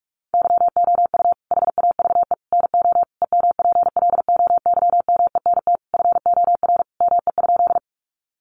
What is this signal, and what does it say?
Morse code, heard as a keyed tone, about 44 wpm, 700 Hz. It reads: YOU HAVE NO EMPLOYMENT FOR ME?
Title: Sentences from Top 1500 Words 45wpm